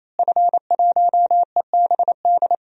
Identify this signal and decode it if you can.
Morse code, heard as a keyed tone, about 28 wpm, 700 Hz. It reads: F1EBD
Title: DXCC-100 Call Signs - 5 Characters - Repeated 3x - 28wpm